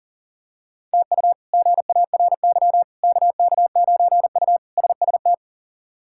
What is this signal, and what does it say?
Morse code, heard as a keyed tone, about 40 wpm, 700 Hz. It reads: TU GARY KK9U SST
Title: K1USN SST QSO Practice - 40wpm